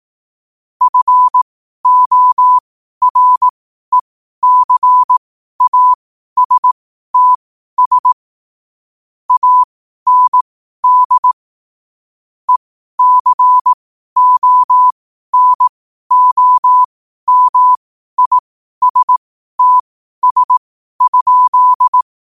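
Morse code, heard as a keyed tone, about 18 wpm, spaced as for 12 wpm, 1000 Hz. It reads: FORECASTS AND ECONOMISTS?
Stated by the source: Morse code quote of the day at 12 wpm.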